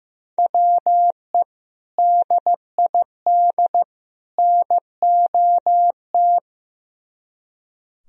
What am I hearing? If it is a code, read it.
Morse code, heard as a keyed tone, about 15 wpm, 700 Hz. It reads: WE DID NOT
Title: Speed-Racing - Sets of 3 Words 15wpm